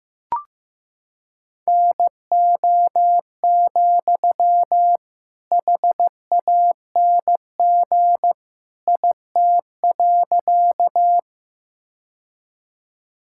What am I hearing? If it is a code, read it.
Morse code, heard as a keyed tone, about 15 wpm, 700 Hz. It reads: NO, HANG IT.
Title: Sentences from Top 1600 Words 15wpm